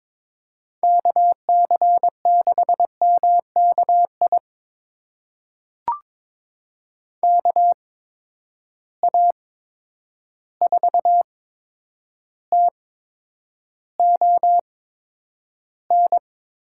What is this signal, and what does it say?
Morse code, heard as a keyed tone, about 22 wpm, spaced as for 9 wpm, 700 Hz. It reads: KC6MKI KA4TON
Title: USA Call Signs - 6 Characters - 8x Character Spacing - 22wpm